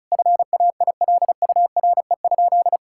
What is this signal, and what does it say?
Morse code, heard as a keyed tone, about 34 wpm, 700 Hz. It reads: FAILURE?
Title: Sentences from Top 1000 Words - Review Part 6 - 35wpm